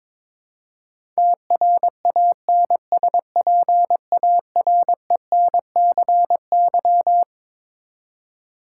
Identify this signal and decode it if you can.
Morse code, heard as a keyed tone, about 22 wpm, 700 Hz. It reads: TRANSPARENCY
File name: Twelve Letter Words - 22wpm